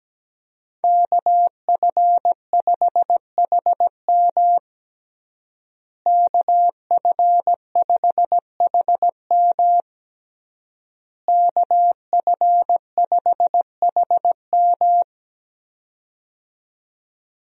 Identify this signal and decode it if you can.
Morse code, heard as a keyed tone, about 17 wpm, 700 Hz. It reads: KF5HM KF5HM KF5HM
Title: USA Call Signs - 5 Characters - Repeated 3x - 17wpm